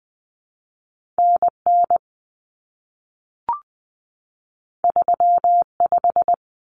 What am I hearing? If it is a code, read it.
Morse code, heard as a keyed tone, about 20 wpm, 700 Hz. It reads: NN 35
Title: Common Cut Numbers - 2-Digit 20wpm